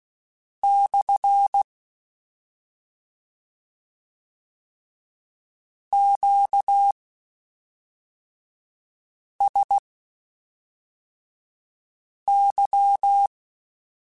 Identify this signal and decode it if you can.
Morse code, heard as a keyed tone, about 16 wpm, spaced as for 3 wpm, 800 Hz. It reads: / QSY
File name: Q Calls / Q Calls-mp3s / Q CALLS SPSL @5